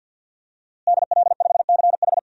Morse code, heard as a keyed tone, about 47 wpm, 700 Hz. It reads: DL5CH
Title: DXCC-100 Call Signs - 5 Characters - Repeated 3x - 50wpm